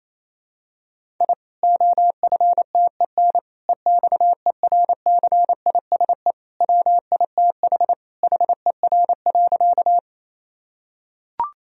Morse code, heard as a keyed tone, about 28 wpm, 700 Hz. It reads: I OFTEN EXERCISE WITH HER.
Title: Sentences from Top 1000 Words - Review Part 4 - 28wpm